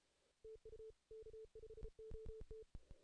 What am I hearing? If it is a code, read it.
Morse code, heard as a keyed tone, about 33 wpm, 440 Hz. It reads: TU K5OT
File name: 2016 JAN CW NAQP / 2016 Jan naqp k5ot - 0-8